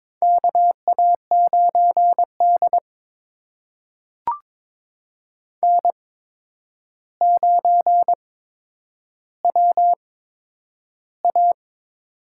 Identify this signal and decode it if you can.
Morse code, heard as a keyed tone, about 22 wpm, spaced as for 9 wpm, 700 Hz. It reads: KA9D N9WA
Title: USA Call Signs - 4 Characters - 8x Character Spacing - 22wpm